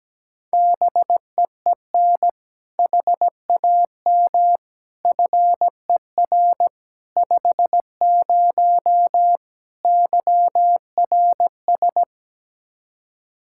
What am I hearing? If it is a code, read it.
Morse code, heard as a keyed tone, about 17 wpm, 700 Hz. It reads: BEEN HAM FER 50 YRS
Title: Practice QSO - Exchange 3 - More Info - 17wpm